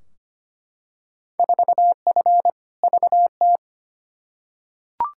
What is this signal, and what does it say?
Morse code, heard as a keyed tone, about 25 wpm, 700 Hz. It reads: 4F VT
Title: Field Day Exchanges Encore - 25wpm